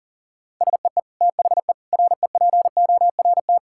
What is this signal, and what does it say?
Morse code, heard as a keyed tone, about 40 wpm, 700 Hz. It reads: SEE THE REPORT
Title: Sentences from Top 1000 Words - Review Part 6 - 40wpm